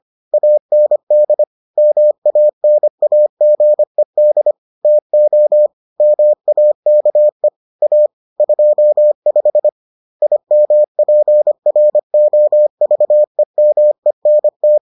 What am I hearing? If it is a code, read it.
Morse code, heard as a keyed tone, about 25 wpm, 600 Hz. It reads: AND MANAGED TO MAKE A 25 IMPROVEMENT